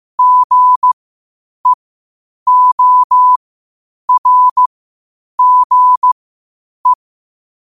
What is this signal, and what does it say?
Morse code, heard as a keyed tone, about 15 wpm, spaced as for 8 wpm, 1000 Hz. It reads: GEORGE
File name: qotdfiles / 8wpm